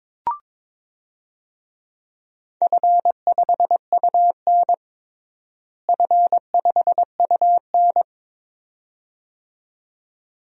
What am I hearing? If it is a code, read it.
Morse code, heard as a keyed tone, about 22 wpm, 700 Hz. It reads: F5UN F5UN
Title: DXCC-100 Call Signs - 4 Characters - Repeated 2x - 22wpm